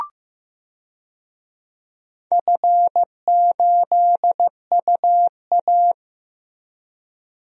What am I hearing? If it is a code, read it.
Morse code, heard as a keyed tone, about 15 wpm, 700 Hz. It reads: F8UA
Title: DXCC-100 Call Signs - 4 Characters - Repeated 3x - 15wpm